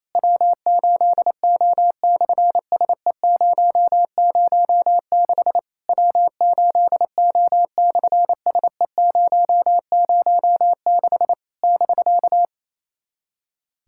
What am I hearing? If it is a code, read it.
Morse code, heard as a keyed tone, about 28 wpm, 700 Hz. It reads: W8O/SE006 W8O/SE006 <BK>
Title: SOTA QSO Practice 28wpm